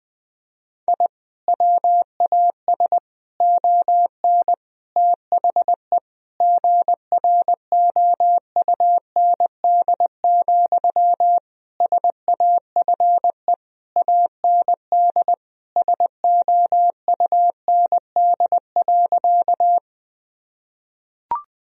Morse code, heard as a keyed tone, about 20 wpm, 700 Hz. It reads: I WAS ON THE GROUND, SAFE AND SOUND.
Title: Sentences from Top 1000 Words - Review Part 2 - 20wpm